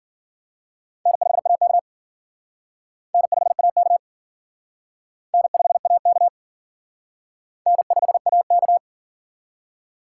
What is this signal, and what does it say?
Morse code, heard as a keyed tone, about 50 wpm, 700 Hz. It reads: N5AK N5AK N5AK N5AK
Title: Speed Racing - USA 1x2 Call Signs 40wpm